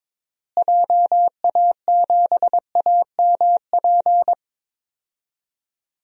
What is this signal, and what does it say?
Morse code, heard as a keyed tone, about 22 wpm, 700 Hz. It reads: JA7AMP
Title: DXCC-100 Call Signs - 6 Characters - Repeated 3x - 22wpm